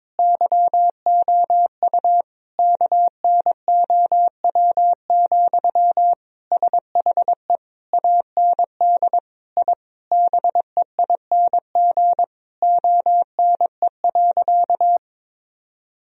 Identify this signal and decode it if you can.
Morse code, heard as a keyed tone, about 22 wpm, 700 Hz. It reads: YOU KNOW, SHE AND I BEING ONE.
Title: Sentences from Top 1000 Words - Review Part 3 - 22wpm